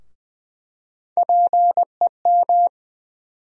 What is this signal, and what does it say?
Morse code, heard as a keyed tone, about 20 wpm, 700 Hz. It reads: PEM
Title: USA Call Sign Postfixes - 3 Letters 20wpm